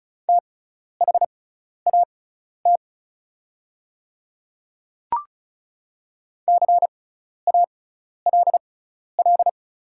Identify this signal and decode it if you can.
Morse code, heard as a keyed tone, about 36 wpm, spaced as for 12 wpm, 700 Hz. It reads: THAT CALL
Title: 6x Character Spacing - Top 100 Words - 35wpm